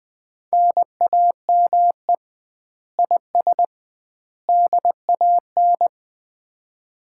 Morse code, heard as a keyed tone, about 20 wpm, 700 Hz. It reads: NAME IS DAN
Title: Common QSO Phrases - 2x Word-Spacing - 20wpm